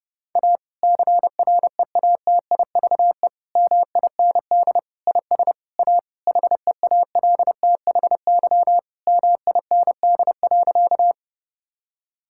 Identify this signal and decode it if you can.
Morse code, heard as a keyed tone, about 32 wpm, 700 Hz. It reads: A CREATIVE MIND IS A HEALTHY MIND.
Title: Sentences from Top 1000 Words - Review Part 5 - 30wpm